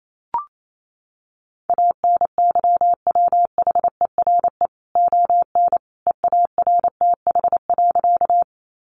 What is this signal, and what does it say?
Morse code, heard as a keyed tone, about 28 wpm, 700 Hz. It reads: ANYWHERE ON EARTH.
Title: Sentences from Top 1400 Words 28wpm